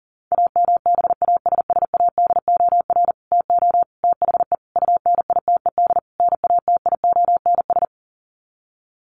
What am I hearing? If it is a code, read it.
Morse code, heard as a keyed tone, about 40 wpm, 700 Hz. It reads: AMBASSADOR TO THE UNITED NATIONS
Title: Sets of 5 Words Encore2 40wpm